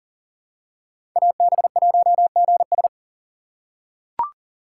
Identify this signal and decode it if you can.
Morse code, heard as a keyed tone, about 40 wpm, 700 Hz. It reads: AB1GS
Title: USA Call Signs - 5 Characters - 4x Character Spacing - 40wpm